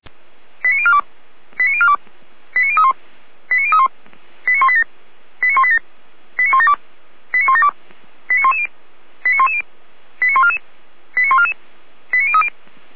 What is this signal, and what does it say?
Five-tone selective call (ZVEI), heard as quick runs of five tones, tones of about 70 ms. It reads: 89042 89042 89031 89031 89187 89187 89172 89172 89110 89110 89144 89144 89030